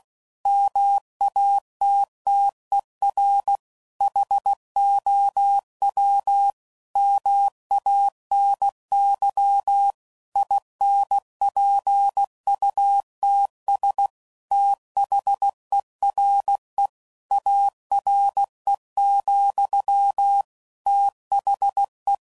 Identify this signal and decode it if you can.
Morse code, heard as a keyed tone, about 16 wpm, 800 Hz. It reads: MATTER HOW MANY INPUTS THERE ARE, THE
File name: CW Code Groups @15WPM / Technical Subjects / OR Gate